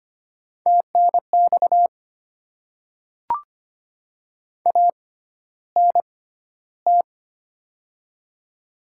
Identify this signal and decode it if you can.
Morse code, heard as a keyed tone, about 25 wpm, spaced as for 13 wpm, 700 Hz. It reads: TNX ANT
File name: Top 20 QSO Elements - 6x Character Spacing - 25wpm